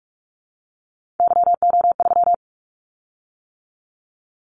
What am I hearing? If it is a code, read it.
Morse code, heard as a keyed tone, about 44 wpm, 700 Hz. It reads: YO3
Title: Callsign Prefixes for Top 25 DXCC - 45wpm